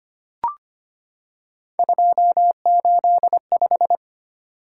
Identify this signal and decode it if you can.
Morse code, heard as a keyed tone, about 25 wpm, 700 Hz. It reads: 285
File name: Three Letter-Number Combinations - 25wpm